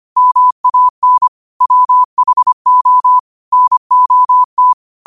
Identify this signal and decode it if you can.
Morse code, heard as a keyed tone, about 25 wpm, 1000 Hz. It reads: MAN WHO NOT
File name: qotdfiles / 25wpm